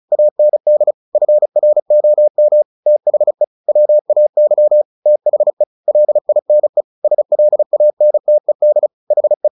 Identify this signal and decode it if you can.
Morse code, heard as a keyed tone, about 34 wpm, 600 Hz. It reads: AND FROM THE WAY THE LINE SLANTED HE
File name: the old man and the sea 2 35wpm